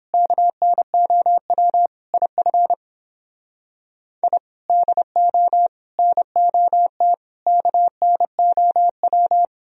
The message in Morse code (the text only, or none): KNOW IF I DO NOT KNOW